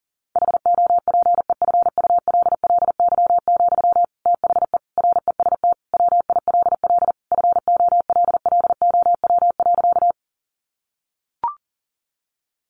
Morse code, heard as a keyed tone, about 40 wpm, 700 Hz. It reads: HOPEFULLY, THE REST WILL FOLLOW.
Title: Sentences from Top 1500 Words 40wpm